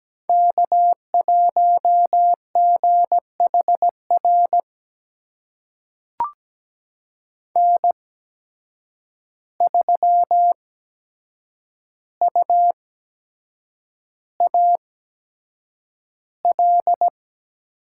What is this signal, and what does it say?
Morse code, heard as a keyed tone, about 17 wpm, spaced as for 7 wpm, 700 Hz. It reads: K1GHR N3UAL
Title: USA Call Signs - 5 Characters - 8x Character Spacing - 17wpm